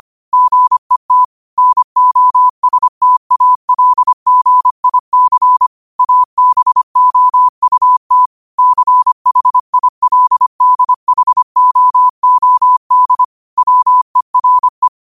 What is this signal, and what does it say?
Morse code, heard as a keyed tone, about 25 wpm, 1000 Hz. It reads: GET NOSTALGIC ABOUT CHILDHOOD WERE
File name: qotdfiles / 25wpm